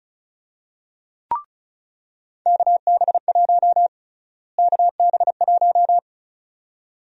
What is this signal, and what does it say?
Morse code, heard as a keyed tone, about 34 wpm, 700 Hz. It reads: KB1 KB1
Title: USA Call Sign Prefixes - Repeated 2x - 35wpm